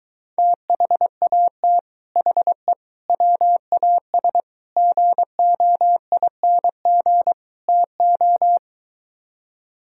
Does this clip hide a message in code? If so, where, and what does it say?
Yes: Morse code from 0.4 to 8.6 s, heard as a keyed tone, about 23 wpm, 700 Hz. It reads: THAT HE WAS GOING TO